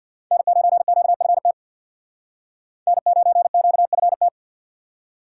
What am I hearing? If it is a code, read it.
Morse code, heard as a keyed tone, about 64 wpm, 700 Hz. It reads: N9QFT N9QFT